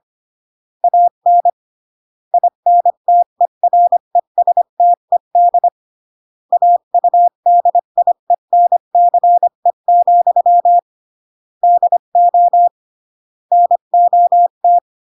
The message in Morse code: AN INTERESTED AUDIENCE, DO NOT